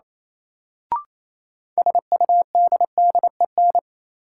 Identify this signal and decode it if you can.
Morse code, heard as a keyed tone, about 27 wpm, 700 Hz. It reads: SUDDEN
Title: Sets of 4 Words - New Word Drill - 28wpm